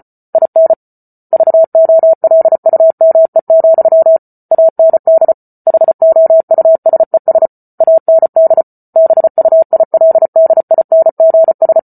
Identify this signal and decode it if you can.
Morse code, heard as a keyed tone, about 34 wpm, 650 Hz. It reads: IN VOLUME, AND HOUSES AND BUILDINGS